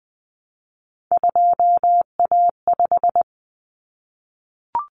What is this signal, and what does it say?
Morse code, heard as a keyed tone, about 20 wpm, 700 Hz. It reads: 2A5
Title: Common Cut Numbers - 3-Digit 20wpm